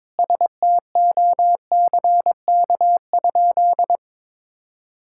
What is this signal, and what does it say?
Morse code, heard as a keyed tone, about 22 wpm, 700 Hz. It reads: STOCK?
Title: Sentences from Top 1000 Words - Review Part 5 - 22wpm